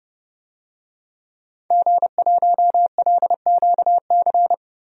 Morse code, heard as a keyed tone, about 30 wpm, 700 Hz. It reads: G1LQC
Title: DXCC-100 Call Signs - 5 Characters - 30wpm